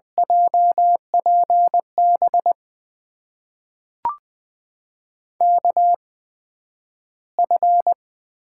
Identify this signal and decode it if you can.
Morse code, heard as a keyed tone, about 20 wpm, spaced as for 11 wpm, 700 Hz. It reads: JPB KF
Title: USA Call Signs - 6 Characters - 8x Character Spacing - 20wpm